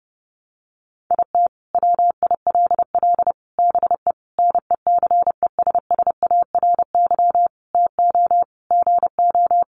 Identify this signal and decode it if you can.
Morse code, heard as a keyed tone, about 30 wpm, 700 Hz. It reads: IT WILL BE NECESSARY TO GO
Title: Sentences from Top 800 Words 30wpm